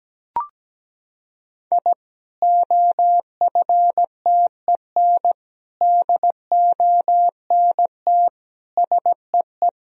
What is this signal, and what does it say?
Morse code, heard as a keyed tone, about 17 wpm, 700 Hz. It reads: I OFTEN DONT SEE
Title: Sentences from Top 1000 Words - Review Part 1 - 17wpm